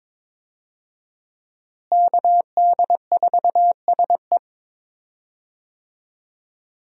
Morse code, heard as a keyed tone, about 22 wpm, 700 Hz. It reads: KD4SE
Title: USA Call Signs - 5 Characters - Repeated 2x - 22wpm